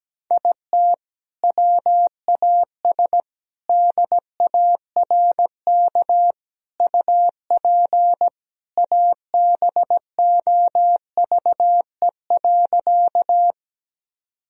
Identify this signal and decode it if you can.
Morse code, heard as a keyed tone, about 17 wpm, 700 Hz. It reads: IT WAS DARK UP ABOVE.
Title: Sentences from Top 1000 Words - Review Part 2 - 17wpm